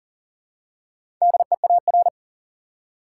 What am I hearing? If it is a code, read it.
Morse code, heard as a keyed tone, about 40 wpm, 700 Hz. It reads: DEAR
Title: Four Letter Words - 40wpm